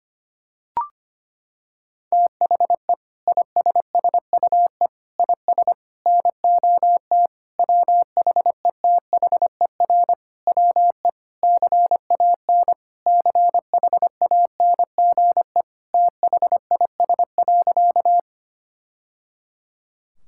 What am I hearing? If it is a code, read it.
Morse code, heard as a keyed tone, about 25 wpm, 700 Hz. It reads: THE ISSUE IS NOT WHETHER WE CAN CHANGE THIS.
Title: Sentences from Top 1000 Words - Review Part 6 - 25wpm